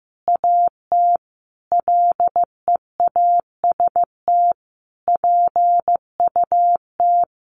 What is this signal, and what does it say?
Morse code, heard as a keyed tone, about 15 wpm, 700 Hz. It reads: AT LEAST PUT
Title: Sentences from Top 1200 Words 15wpm